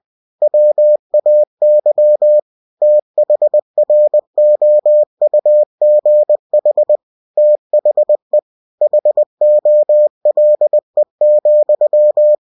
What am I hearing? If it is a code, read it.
Morse code, heard as a keyed tone, about 20 wpm, 600 Hz. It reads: WAY THROUGH THE HOLE,
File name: cw-books / winnie-the-pooh / winnie-20wpm-page-0015